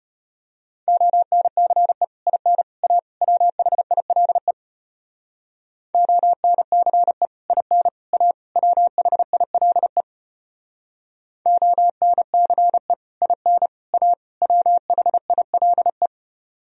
Morse code, heard as a keyed tone, about 33 wpm, 700 Hz. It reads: ONCE IN A WHILE ONCE IN A WHILE ONCE IN A WHILE